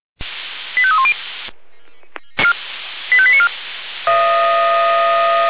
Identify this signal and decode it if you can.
Five-tone selective call (ZVEI), heard as quick runs of five tones, tones of about 70 ms. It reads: 85310 85894
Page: ZVEI